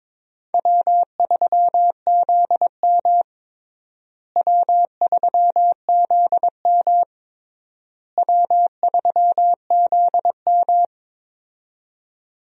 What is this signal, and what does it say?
Morse code, heard as a keyed tone, about 22 wpm, 700 Hz. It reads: W3ZM W3ZM W3ZM
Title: USA Call Signs - 4 Characters - Repeated 3x - 22wpm